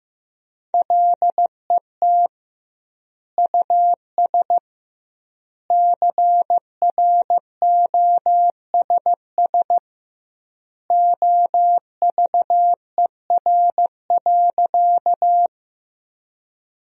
Morse code, heard as a keyed tone, about 15 wpm, 700 Hz. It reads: LET US CROSS OVER.
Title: 2x Word-Spacing - Sentences from Top 300 Words - 15wpm